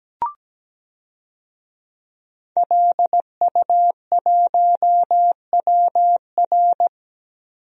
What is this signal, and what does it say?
Morse code, heard as a keyed tone, about 17 wpm, 700 Hz. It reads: LU1WR
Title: DXCC-100 Call Signs - 5 Characters - Repeated 3x - 17wpm